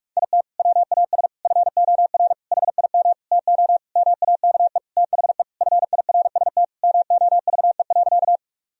Morse code, heard as a keyed tone, about 42 wpm, 700 Hz. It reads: IT WAS FOR HIM TO MAKE THE FIRST MOVE.
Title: Sentences from Top 1000 Words - Review Part 1 - 45wpm